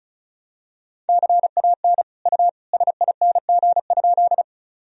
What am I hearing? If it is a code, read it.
Morse code, heard as a keyed tone, about 36 wpm, 700 Hz. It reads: CAN U SING?
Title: Sentences from Top 400 Words and Abbreviations 35wpm